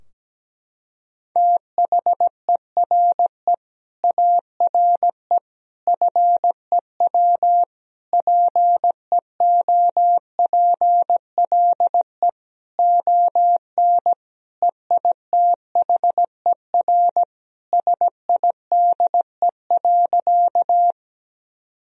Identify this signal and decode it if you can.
Morse code, heard as a keyed tone, about 17 wpm, 700 Hz. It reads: THERE ARE FEW PEOPLE ON EITHER SIDE.
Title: Sentences from Top 1000 Words - Review Part 4 - 17wpm